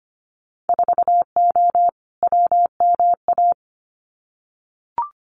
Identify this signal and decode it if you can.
Morse code, heard as a keyed tone, about 25 wpm, 700 Hz. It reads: 4O WMA
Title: Winter Field Day Exchanges Encore 25wpm